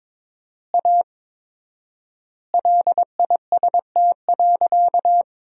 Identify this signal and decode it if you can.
Morse code, heard as a keyed tone, about 23 wpm, 700 Hz. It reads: A LIST.
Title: 4x Word-Spacing - Sentences from Top 400 Words - 22wpm